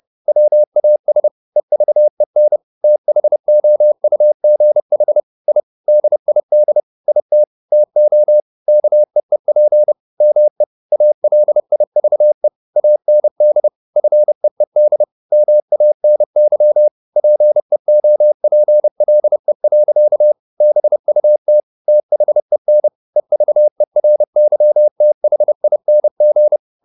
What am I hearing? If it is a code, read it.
Morse code, heard as a keyed tone, about 30 wpm, 600 Hz. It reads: WAS EVEN THOUGH I DID IT TO KEEP ME ALIVE AND FEED MANY PEOPLE. BUT THEN EVERYTHING